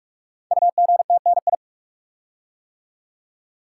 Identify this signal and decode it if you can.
Morse code, heard as a keyed tone, about 44 wpm, 700 Hz. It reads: UGTNI
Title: Code Groups - 5 Letters - 45wpm